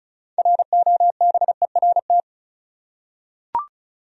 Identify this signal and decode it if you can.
Morse code, heard as a keyed tone, about 36 wpm, 700 Hz. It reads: ROBERT
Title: USA Common Names - 35wpm